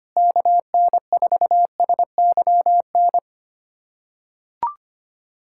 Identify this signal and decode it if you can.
Morse code, heard as a keyed tone, about 25 wpm, 700 Hz. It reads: KN4SYN
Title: USA Callsign DE Callsign - 6 Characters - 25wpm